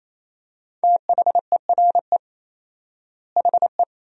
Morse code, heard as a keyed tone, about 28 wpm, 700 Hz. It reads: THERE HE
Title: 4x Word-Spacing - Sentences from Top 100 Words - 28wpm